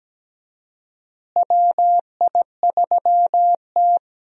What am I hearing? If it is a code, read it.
Morse code, heard as a keyed tone, about 17 wpm, 700 Hz. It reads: WI3T
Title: USA Call Signs - 4 Characters - Repeated 3x - 17wpm